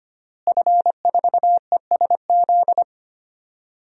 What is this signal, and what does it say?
Morse code, heard as a keyed tone, about 25 wpm, 700 Hz. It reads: F4ESZ
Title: DXCC-100 Call Signs - 5 Characters - Repeated 2x - 25wpm